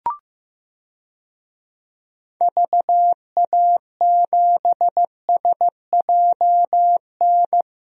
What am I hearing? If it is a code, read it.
Morse code, heard as a keyed tone, about 16 wpm, 700 Hz. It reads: VA7SJN
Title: DXCC-100 Call Signs - 6 Characters - Repeated 3x - 15wpm